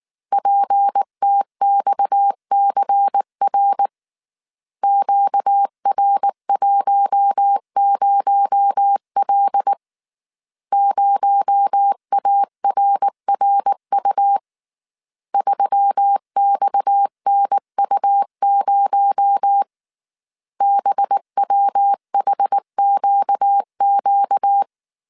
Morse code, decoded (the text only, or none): PTXCR QR10L 0ARRU 3XNU0 BWHQQ